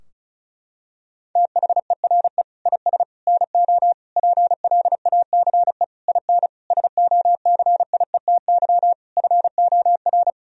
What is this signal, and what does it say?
Morse code, heard as a keyed tone, about 36 wpm, 700 Hz. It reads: THERE IS NO PLACE IN SOCIETY FOR
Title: Sentences from Top 1000 Words - Review Part 4 - 35wpm